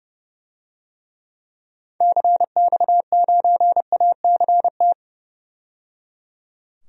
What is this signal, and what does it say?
Morse code, heard as a keyed tone, about 30 wpm, 700 Hz. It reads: CX9ACT
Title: DXCC-100 Call Signs - 6 Characters - 30wpm